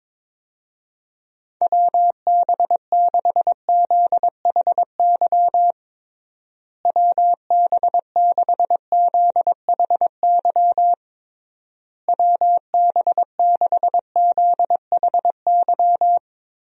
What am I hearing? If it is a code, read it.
Morse code, heard as a keyed tone, about 22 wpm, 700 Hz. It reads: WB6ZHY WB6ZHY WB6ZHY
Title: USA Call Signs - 6 Characters - Repeated 3x - 22wpm